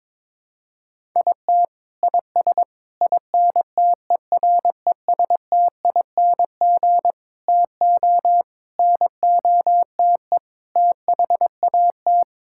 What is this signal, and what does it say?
Morse code, heard as a keyed tone, about 22 wpm, 700 Hz. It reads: IT IS INTERESTING TO NOTE THAT
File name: practice / 22wpm / Sets of 6 Words - 22wpm